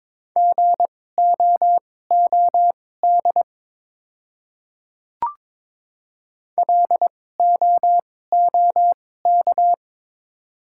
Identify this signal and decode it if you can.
Morse code, heard as a keyed tone, about 22 wpm, spaced as for 16 wpm, 700 Hz. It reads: GOOD LOOK
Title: 2x Character Spacing - Top 200 Words - 22wpm